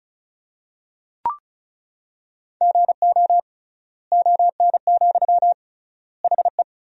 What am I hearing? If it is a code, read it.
Morse code, heard as a keyed tone, about 34 wpm, 700 Hz. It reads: GO ON, HE
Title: 3x Word-Spacing - Sentences from Top 100 Words - 35wpm